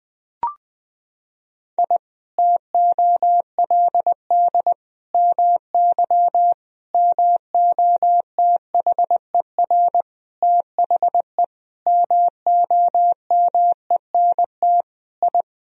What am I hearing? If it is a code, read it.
Morse code, heard as a keyed tone, about 21 wpm, 700 Hz. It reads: I TOLD MY MOTHER THE MOMENT I
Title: Sentences from Top 1000 Words - Review Part 1 - 20wpm